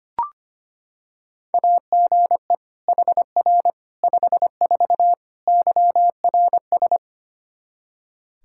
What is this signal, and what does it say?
Morse code, heard as a keyed tone, about 25 wpm, 700 Hz. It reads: AGE HR 54 YRS
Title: Practice QSO - Exchange 3 - More Info - 25wpm